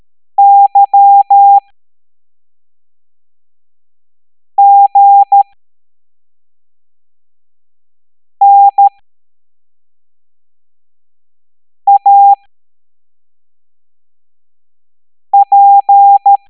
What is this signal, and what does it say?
Morse code, heard as a keyed tone, about 13 wpm, spaced as for 3 wpm, 800 Hz. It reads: YGNAP